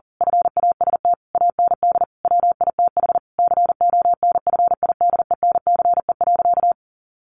Morse code, heard as a keyed tone, about 40 wpm, 700 Hz. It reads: FAST AND WITH CONFIDENCE.
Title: Sentences from Top 1500 Words 40wpm